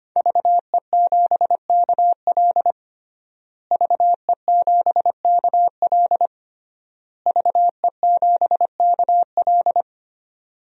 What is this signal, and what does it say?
Morse code, heard as a keyed tone, about 25 wpm, 700 Hz. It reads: VE7KL VE7KL VE7KL
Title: DXCC-100 Call Signs - 5 Characters - Repeated 3x - 25wpm